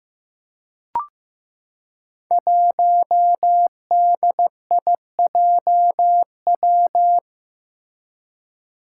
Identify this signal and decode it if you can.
Morse code, heard as a keyed tone, about 16 wpm, 700 Hz. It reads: 1DIJW
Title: Five Challenging Letter-Number Combinations 15wpm